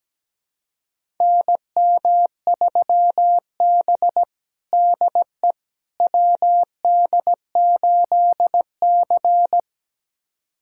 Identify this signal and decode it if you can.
Morse code, heard as a keyed tone, about 17 wpm, 700 Hz. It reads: NM3B DE WD8C
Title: USA Callsign DE Callsign - 4 Characters - 17wpm